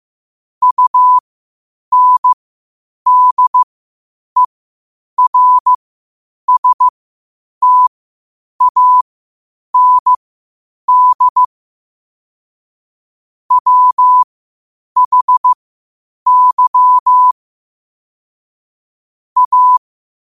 Morse code, heard as a keyed tone, about 15 wpm, spaced as for 8 wpm, 1000 Hz. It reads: UNDERSTAND WHY A
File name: qotdfiles / 8wpm